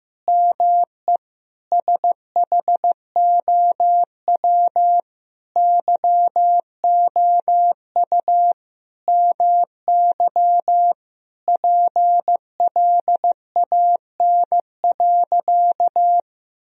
Morse code, heard as a keyed tone, about 16 wpm, 700 Hz. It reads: ME SHOW YOU MY PLAN.
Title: Sentences from Top 1000 Words - Review Part 2 - 15wpm